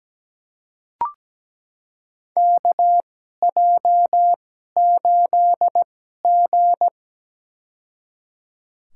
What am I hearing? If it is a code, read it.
Morse code, heard as a keyed tone, about 17 wpm, spaced as for 12 wpm, 700 Hz. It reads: KJ8G
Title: USA Call Signs - 2x Character Spacing - 17wpm